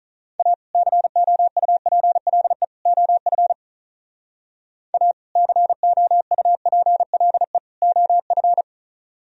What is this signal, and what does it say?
Morse code, heard as a keyed tone, about 44 wpm, 700 Hz. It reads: A COUPLE OF A COUPLE OF